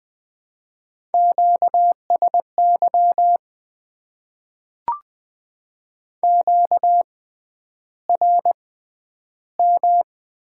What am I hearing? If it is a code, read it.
Morse code, heard as a keyed tone, about 20 wpm, spaced as for 10 wpm, 700 Hz. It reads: QSY QRM
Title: Top 90 QSO Elements - 6x Character Spacing - 20wpm